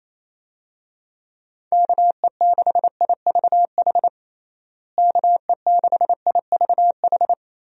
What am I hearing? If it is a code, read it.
Morse code, heard as a keyed tone, about 28 wpm, 700 Hz. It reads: KE6IVH KE6IVH
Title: USA Call Signs - 6 Characters - Repeated 2x - 28wpm